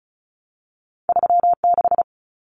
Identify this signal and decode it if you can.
Morse code, heard as a keyed tone, about 36 wpm, 700 Hz. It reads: 36
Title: Common Cut Numbers - 2-Digit 35wpm